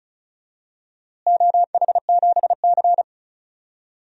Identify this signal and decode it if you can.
Morse code, heard as a keyed tone, about 34 wpm, 700 Hz. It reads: OH7C
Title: DXCC-100 Call Signs - 4 Characters - Repeated 3x - 35wpm